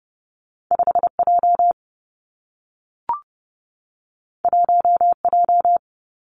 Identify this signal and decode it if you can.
Morse code, heard as a keyed tone, about 30 wpm, 700 Hz. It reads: 5 J 1 J
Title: Two Challenging Letter-Number Combinations 30wpm